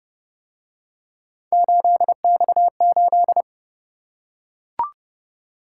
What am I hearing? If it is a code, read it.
Morse code, heard as a keyed tone, about 31 wpm, 700 Hz. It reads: 8X8